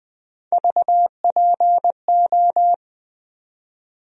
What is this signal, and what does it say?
Morse code, heard as a keyed tone, about 20 wpm, 700 Hz. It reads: VPO
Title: USA Call Sign Postfixes - 3 Letters 20wpm